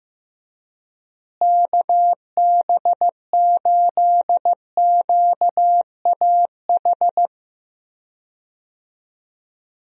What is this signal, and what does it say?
Morse code, heard as a keyed tone, about 15 wpm, 700 Hz. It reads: KB8QAH
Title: USA Call Signs - 6 Characters - Repeated 2x - 15wpm